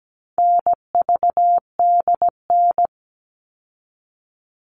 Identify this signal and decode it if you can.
Morse code, heard as a keyed tone, about 17 wpm, 700 Hz. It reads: NVDN
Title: All Cut Numbers - 4-Digit 17wpm